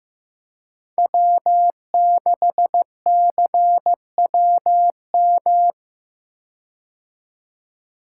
Morse code, heard as a keyed tone, about 16 wpm, 700 Hz. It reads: W6CWM
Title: K1USN SST QSO Practice - 15wpm